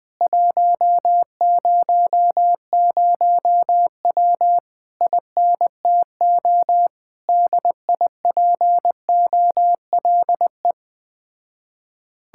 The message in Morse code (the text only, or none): 100W INTO DIPOLE